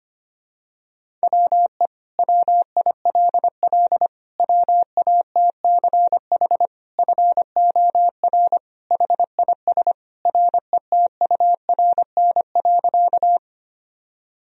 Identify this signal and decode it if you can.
Morse code, heard as a keyed tone, about 25 wpm, 700 Hz. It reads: WE WILL WATCH FOR HIS RETURN.
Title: Sentences from Top 1000 Words - Review Part 1 - 25wpm